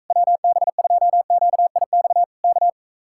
Morse code, heard as a keyed tone, about 44 wpm, 700 Hz. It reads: WB2QIX K